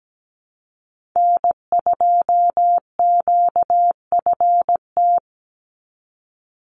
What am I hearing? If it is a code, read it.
Morse code, heard as a keyed tone, about 17 wpm, 700 Hz. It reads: N2QFT
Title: USA Call Signs 17wpm